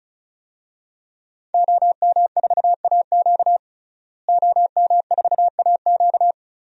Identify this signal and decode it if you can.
Morse code, heard as a keyed tone, about 36 wpm, 700 Hz. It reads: OM4AQ OM4AQ
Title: DXCC-100 Call Signs - 5 Characters - Repeated 3x - 35wpm